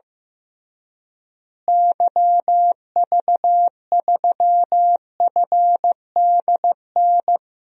Morse code, heard as a keyed tone, about 16 wpm, 700 Hz. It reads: YV3FDN
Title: DXCC-100 Call Signs - 6 Characters - Repeated 3x - 15wpm